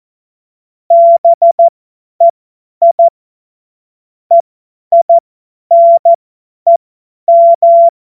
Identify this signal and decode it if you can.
Morse code, heard as a keyed tone, about 14 wpm, spaced as for 10 wpm, 680 Hz. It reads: BEI EINEM
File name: Morsekurs / Witze / Witz 17. Januar 2023 50 F70